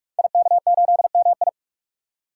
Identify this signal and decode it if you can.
Morse code, heard as a keyed tone, about 44 wpm, 700 Hz. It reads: IK8MI
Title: Code Groups - 5 Letter:Numbers - 45wpm